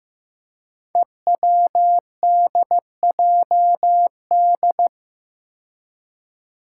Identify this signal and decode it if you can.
Morse code, heard as a keyed tone, about 15 wpm, 700 Hz. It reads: EWDJD
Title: Five Challenging Letter-Number Combinations 15wpm